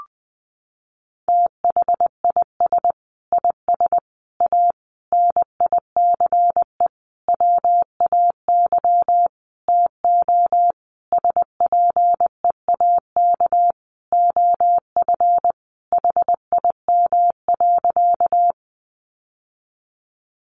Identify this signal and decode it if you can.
Morse code, heard as a keyed tone, about 20 wpm, 700 Hz. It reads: THIS IS A NICE WAY TO SPEAK OF HIM.
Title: Sentences from Top 1100 Words 20wpm